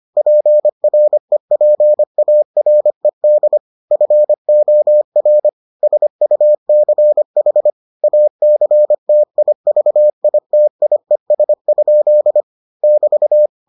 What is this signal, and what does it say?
Morse code, heard as a keyed tone, about 25 wpm, 600 Hz. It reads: PREPARED FOR SUCH ACTIVITIES? <BT>